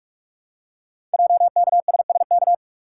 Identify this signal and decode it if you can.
Morse code, heard as a keyed tone, about 44 wpm, 700 Hz. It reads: JKSSK
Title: Code Groups - 5 Letter:Numbers - 45wpm